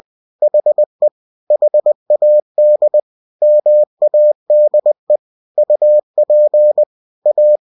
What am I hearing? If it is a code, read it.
Morse code, heard as a keyed tone, about 20 wpm, 600 Hz. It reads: HE HAD MADE UP A